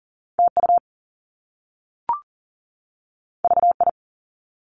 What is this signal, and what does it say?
Morse code, heard as a keyed tone, about 41 wpm, 700 Hz. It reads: TU VI
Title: Top Two Letter Combinations in Words 40wpm